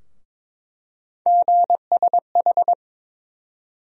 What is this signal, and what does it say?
Morse code, heard as a keyed tone, about 22 wpm, 700 Hz. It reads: GSH
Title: USA Call Sign Postfixes - 3 Letters 22wpm